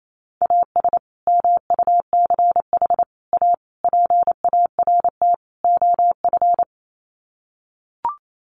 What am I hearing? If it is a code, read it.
Morse code, heard as a keyed tone, about 28 wpm, 700 Hz. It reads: AS MUCH A PART OF
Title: Sets of 5 Words Encore2 28wpm